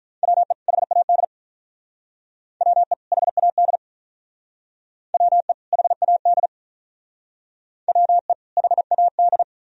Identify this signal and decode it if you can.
Morse code, heard as a keyed tone, about 46 wpm, 700 Hz. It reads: WE HAD WE HAD WE HAD WE HAD